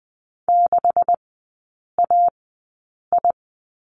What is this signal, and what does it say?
Morse code, heard as a keyed tone, about 20 wpm, spaced as for 8 wpm, 700 Hz. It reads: 6AI
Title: 184 Lesson 30e - Letter b - Three Characters 20wpm